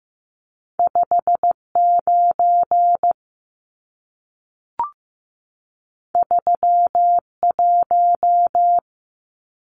Morse code, heard as a keyed tone, about 15 wpm, 700 Hz. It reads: 59 31
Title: Numbers - 2-Digit 15wpm